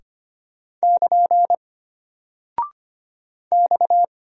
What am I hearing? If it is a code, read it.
Morse code, heard as a keyed tone, about 25 wpm, 700 Hz. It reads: <KN> X
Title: Single Character (Letter:Number:Punctuation:Prosign) Alt 25wpm